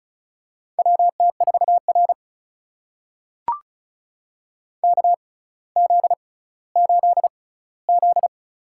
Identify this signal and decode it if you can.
Morse code, heard as a keyed tone, about 36 wpm, spaced as for 18 wpm, 700 Hz. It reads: WT4R KZ8Z
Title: USA Call Signs - 4 Characters - 6x Character Spacing - 35wpm